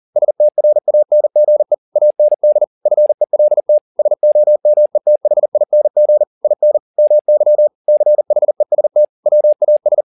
Morse code, heard as a keyed tone, about 40 wpm, 600 Hz. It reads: STRANGE AND FELT SOMETHING IN MY CHEST WAS
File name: the old man and the sea 2 40wpm